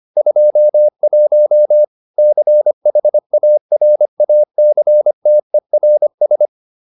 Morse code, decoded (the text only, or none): 21 CHARACTERS